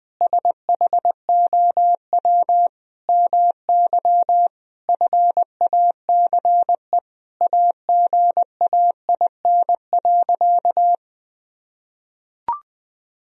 Morse code, heard as a keyed tone, about 21 wpm, 700 Hz. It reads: SHOW MY FACE AGAIN.
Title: Sentences from Top 1000 Words - Review Part 2 - 20wpm